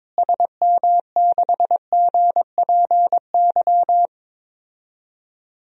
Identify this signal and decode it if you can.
Morse code, heard as a keyed tone, about 22 wpm, 700 Hz. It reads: SM6GPY
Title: DXCC-100 Call Signs - 6 Characters - Repeated 3x - 22wpm